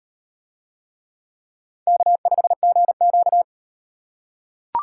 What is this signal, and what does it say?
Morse code, heard as a keyed tone, about 37 wpm, 700 Hz. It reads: K5GQ